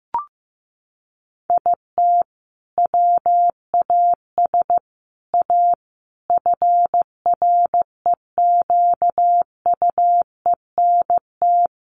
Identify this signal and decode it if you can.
Morse code, heard as a keyed tone, about 14 wpm, 700 Hz. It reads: IT WAS A FREQUENT
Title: Sentences from Top 1400 Words 15wpm